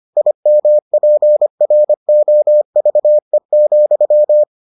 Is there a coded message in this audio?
Yes: Morse code from 0.2 to 4.4 s, heard as a keyed tone, about 25 wpm, 600 Hz. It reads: IMPROVE,